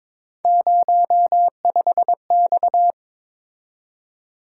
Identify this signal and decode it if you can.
Morse code, heard as a keyed tone, about 26 wpm, 700 Hz. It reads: 05X
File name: Speed-Racing - Three Letter-Number Combinations - 22wpm